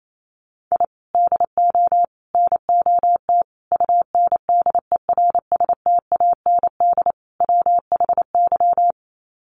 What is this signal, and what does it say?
Morse code, heard as a keyed tone, about 28 wpm, 700 Hz. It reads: I DO NOT UNDERSTAND WHY
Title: Sets of 5 Words Encore 28wpm